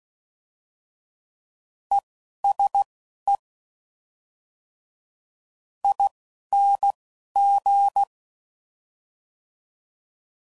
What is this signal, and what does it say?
Morse code, heard as a keyed tone, about 16 wpm, spaced as for 11 wpm, 800 Hz. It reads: ESE ING
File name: Suffixes / General / ESE INGs